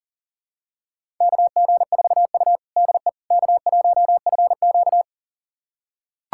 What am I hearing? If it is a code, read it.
Morse code, heard as a keyed tone, about 40 wpm, 700 Hz. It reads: KG4U DE K1FQ
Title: USA Callsign DE Callsign - 4 Characters - 40wpm